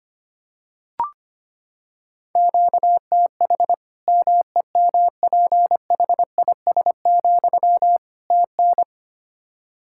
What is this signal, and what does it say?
Morse code, heard as a keyed tone, about 25 wpm, 700 Hz. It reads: QTH MEMPHIS, TN
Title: Practice QSO - Exchange 1 - Beginning - 25wpm